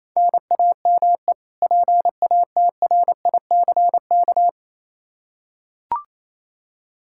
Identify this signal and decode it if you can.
Morse code, heard as a keyed tone, about 28 wpm, 700 Hz. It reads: NAME PATRICK
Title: Practice QSO - Exchange 1 - Beginning - 28wpm